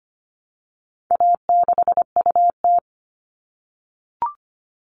Morse code, heard as a keyed tone, about 25 wpm, 700 Hz. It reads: A6UT
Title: All Cut Numbers - 4-Digit 25wpm